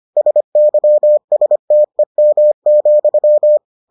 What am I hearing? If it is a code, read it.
Morse code, heard as a keyed tone, about 25 wpm, 600 Hz. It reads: SYSTEM,